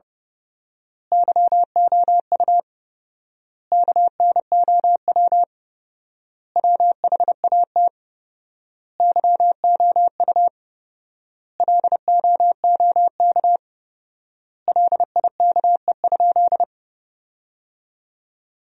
Morse code, heard as a keyed tone, about 30 wpm, 700 Hz. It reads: YOU KNOW WHAT YOU LOOK LIKE?
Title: 4x Word-Spacing - Sentences from Top 100 Words - 30wpm